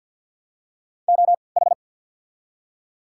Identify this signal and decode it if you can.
Morse code, heard as a keyed tone, about 53 wpm, spaced as for 29 wpm, 690 Hz. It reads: OH
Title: 3x Character Spacing - Top 500 Words - 50wpm